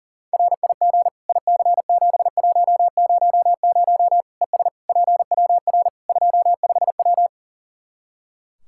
Morse code, heard as a keyed tone, about 40 wpm, 700 Hz. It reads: RIG IC7100 ES PWR 25W
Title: K1USN SST QSO Practice 40wpm